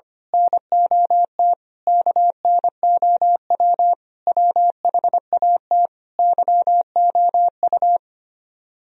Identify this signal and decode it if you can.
Morse code, heard as a keyed tone, about 24 wpm, 700 Hz. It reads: NOT KNOW WHAT YOU
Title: Speed-Racing - Sets of 5 Words 25wpm